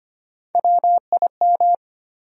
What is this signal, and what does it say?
Morse code, heard as a keyed tone, about 25 wpm, 700 Hz. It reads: WIM
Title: USA Call Sign Postfixes - 3 Letters 25wpm